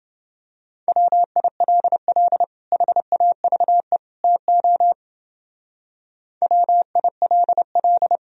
Morse code, heard as a keyed tone, about 29 wpm, 700 Hz. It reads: WILL HAVE TO WILL